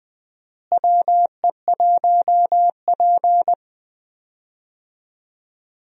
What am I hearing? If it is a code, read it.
Morse code, heard as a keyed tone, about 21 wpm, 700 Hz. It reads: WE1P
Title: USA Call Signs - 4 Characters - Repeated 3x - 20wpm